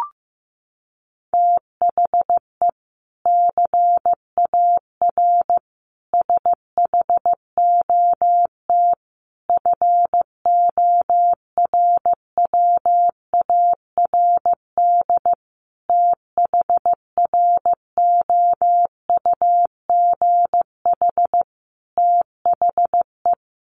Morse code, heard as a keyed tone, about 15 wpm, 700 Hz. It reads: THE CAR SHOT FORWARD THROUGH THE
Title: Sentences from Top 1300 Words 15wpm